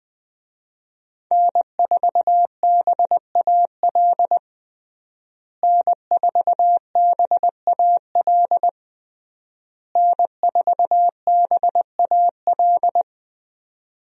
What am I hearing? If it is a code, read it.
Morse code, heard as a keyed tone, about 20 wpm, 700 Hz. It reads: N4BAL N4BAL N4BAL
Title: USA Call Signs - 5 Characters - Repeated 3x - 20wpm